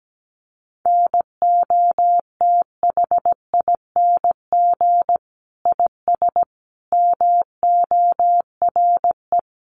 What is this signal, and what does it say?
Morse code, heard as a keyed tone, about 17 wpm, 700 Hz. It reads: NOTHING IS MORE
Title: Sentences from Top 500 Words 17wpm